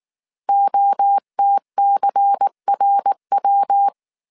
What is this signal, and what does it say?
Morse code, heard as a keyed tone, about 19 wpm, 800 Hz. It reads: OTCRW